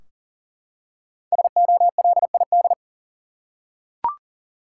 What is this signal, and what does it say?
Morse code, heard as a keyed tone, about 40 wpm, 700 Hz. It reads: SOLID
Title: Five Letter Words - 40wpm